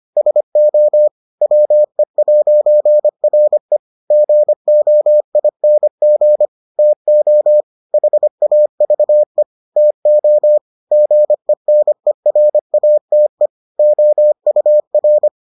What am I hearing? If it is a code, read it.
Morse code, heard as a keyed tone, about 25 wpm, 600 Hz. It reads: SO WE'RE GOING TO HAVE TO GENERATE OUR